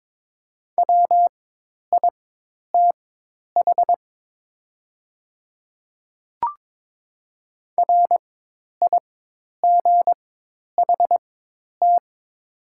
Morse code, heard as a keyed tone, about 22 wpm, spaced as for 10 wpm, 700 Hz. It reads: WITH RIGHT
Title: 4x Character Spacing - Top 300 Words - 22wpm